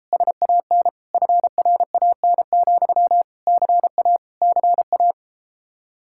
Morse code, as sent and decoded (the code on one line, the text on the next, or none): ... .- -. / ..-. .-. .- -. --..-- / -.-. .- / -.-. .-
SAN FRAN, CA CA